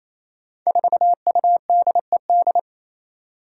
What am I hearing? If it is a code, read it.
Morse code, heard as a keyed tone, about 27 wpm, 700 Hz. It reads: 4UDED
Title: Five Challenging Letter-Number Combinations 28wpm